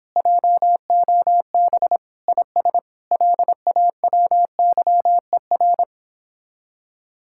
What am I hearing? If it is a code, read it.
Morse code, heard as a keyed tone, about 29 wpm, 700 Hz. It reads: JOB IS LAWYER